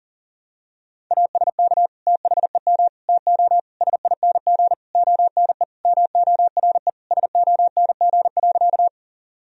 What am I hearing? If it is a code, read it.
Morse code, heard as a keyed tone, about 40 wpm, 700 Hz. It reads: ASK THEM TO SING ONE MORE SONG.
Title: Sentences from Top 1000 Words - Review Part 2 - 40wpm